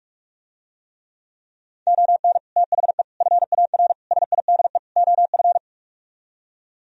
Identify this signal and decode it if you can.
Morse code, heard as a keyed tone, about 48 wpm, 700 Hz. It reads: ON THE FAR SIDE OF